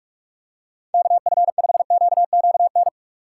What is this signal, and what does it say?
Morse code, heard as a keyed tone, about 47 wpm, 700 Hz. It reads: KF5QQN
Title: USA Call Signs - 6 Characters - 8x Character Spacing - 45wpm